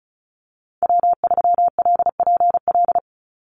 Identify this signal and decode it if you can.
Morse code, heard as a keyed tone, about 34 wpm, 700 Hz. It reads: W3LPL
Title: Call Signs Repeated 3x - 35wpm